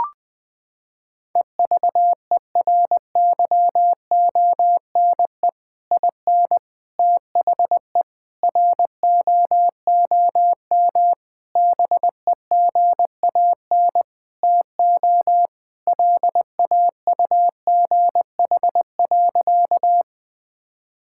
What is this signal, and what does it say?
Morse code, heard as a keyed tone, about 20 wpm, 700 Hz. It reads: EVERYONE IN THE ROOM BEGAN TO LAUGH.
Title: Sentences from Top 1000 Words - Review Part 5 - 20wpm